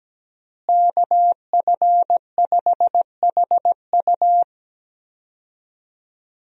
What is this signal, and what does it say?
Morse code, heard as a keyed tone, about 17 wpm, 700 Hz. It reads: KF5HU
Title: USA Call Signs - 5 Characters - Repeated 2x - 17wpm